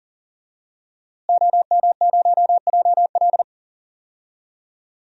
Morse code, heard as a keyed tone, about 40 wpm, 700 Hz. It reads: OM0JL
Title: DXCC-100 Call Signs - 5 Characters - Repeated 3x - 40wpm